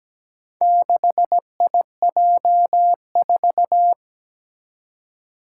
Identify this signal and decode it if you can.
Morse code, heard as a keyed tone, about 17 wpm, 700 Hz. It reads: 6IJ4
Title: Four Challenging Letter-Number Combinations 17wpm